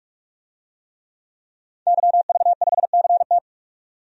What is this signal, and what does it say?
Morse code, heard as a keyed tone, about 46 wpm, 700 Hz. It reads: YV5CT
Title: DXCC-100 Call Signs - 5 Characters - 45wpm